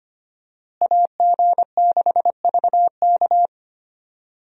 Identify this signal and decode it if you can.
Morse code, heard as a keyed tone, about 25 wpm, 700 Hz. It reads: AG6VK
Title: USA Call Signs - 5 Characters - Repeated 3x - 25wpm